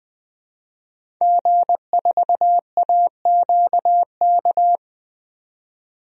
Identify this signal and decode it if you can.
Morse code, heard as a keyed tone, about 20 wpm, 700 Hz. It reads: G4AQK
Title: DXCC-100 Call Signs - 5 Characters - 20wpm